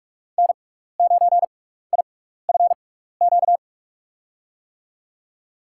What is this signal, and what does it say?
Morse code, heard as a keyed tone, about 46 wpm, 700 Hz. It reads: N 9 I F Q
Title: USA Call Signs - 5 Characters - 6x Character Spacing - 45wpm